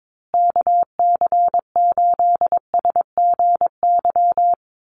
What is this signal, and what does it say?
Morse code, heard as a keyed tone, about 22 wpm, 700 Hz. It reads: KC8SGY
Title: USA Call Signs Repeated 2x 22wpm